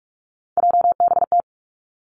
Morse code, heard as a keyed tone, about 46 wpm, 700 Hz. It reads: JBT
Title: Three Challenging Letter-Number Combinations 45wpm